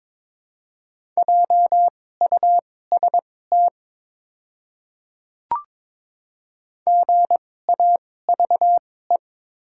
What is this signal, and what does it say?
Morse code, heard as a keyed tone, about 22 wpm, spaced as for 16 wpm, 700 Hz. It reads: JUST GAVE
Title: 2x Character Spacing - Top 500 Words - 22wpm